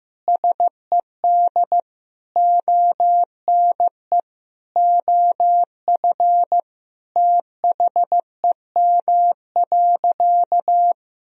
Morse code, heard as a keyed tone, about 15 wpm, 700 Hz. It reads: SED ONE OF THEM.
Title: Sentences from Top 300 Words and Abbreviations 15wpm